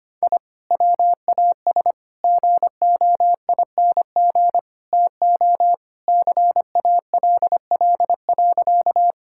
Morse code, heard as a keyed tone, about 25 wpm, 700 Hz. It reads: I WAS GOING TO CALL.
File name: practice / 25wpm / Sentences with 4-Word Phrases - 25wpm